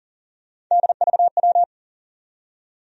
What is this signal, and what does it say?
Morse code, heard as a keyed tone, about 40 wpm, 700 Hz. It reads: DVW
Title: USA Call Sign Postfixes - 3 Letters 40wpm